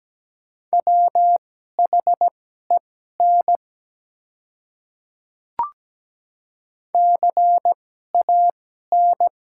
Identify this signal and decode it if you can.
Morse code, heard as a keyed tone, about 17 wpm, spaced as for 12 wpm, 700 Hz. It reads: WHEN CAN
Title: 2x Character Spacing - Top 100 Words - 17wpm